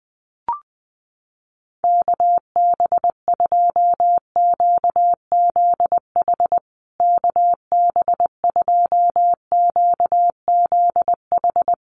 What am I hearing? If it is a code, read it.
Morse code, heard as a keyed tone, about 19 wpm, 700 Hz. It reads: KB2QZH KB2QZH
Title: USA Call Signs Repeated 2x 20wpm